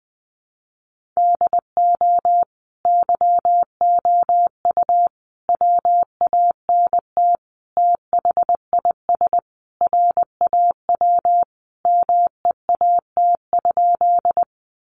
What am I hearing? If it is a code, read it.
Morse code, heard as a keyed tone, about 20 wpm, 700 Hz. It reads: DO YOU WANT THIS RAW MEAT?
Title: Sentences from Top 1500 Words 20wpm